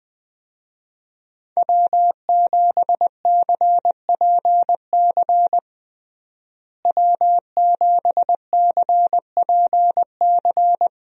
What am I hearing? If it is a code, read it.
Morse code, heard as a keyed tone, about 20 wpm, 700 Hz. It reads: W7CPC W7CPC
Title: USA Call Signs - 5 Characters - Repeated 2x - 20wpm